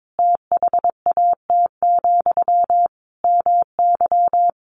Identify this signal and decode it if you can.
Morse code, heard as a keyed tone, about 22 wpm, 700 Hz. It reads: THAT, MY
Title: Sentences from Top 1100 Words 22wpm